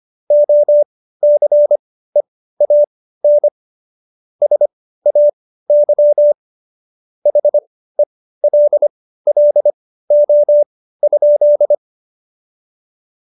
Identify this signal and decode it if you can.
Morse code, heard as a keyed tone, about 25 wpm, spaced as for 15 wpm, 590 Hz. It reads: OCEAN SAY HELLO?
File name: Jokes-25-char-15-eff